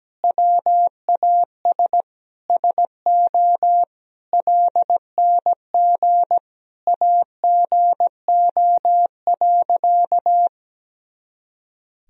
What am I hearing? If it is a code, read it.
Morse code, heard as a keyed tone, about 17 wpm, 700 Hz. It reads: WAS SO LNG AGO.
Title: Sentences from Top 500 Words and Abbreviations 17wpm